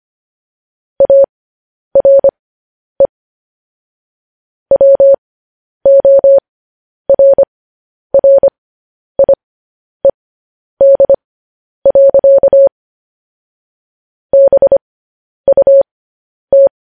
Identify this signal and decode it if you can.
Morse code, heard as a keyed tone, about 25 wpm, spaced as for 10 wpm, 550 Hz. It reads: ARE WORRIED. BUT